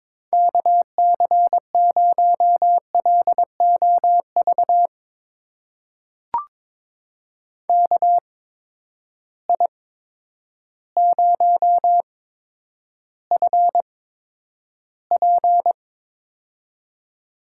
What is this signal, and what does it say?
Morse code, heard as a keyed tone, about 22 wpm, spaced as for 10 wpm, 700 Hz. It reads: KC0LOV KI0FP
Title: USA Call Signs - 8x Character Spacing - 22wpm